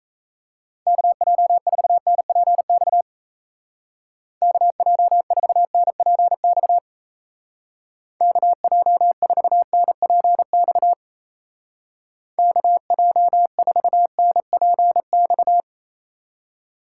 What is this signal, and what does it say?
Morse code, heard as a keyed tone, about 36 wpm, 700 Hz. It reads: KJ4NPX KJ4NPX KJ4NPX KJ4NPX